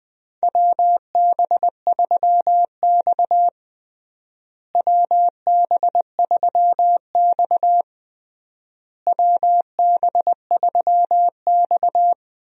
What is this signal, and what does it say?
Morse code, heard as a keyed tone, about 20 wpm, 700 Hz. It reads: WB3X WB3X WB3X
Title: USA Call Signs - 4 Characters - Repeated 3x - 20wpm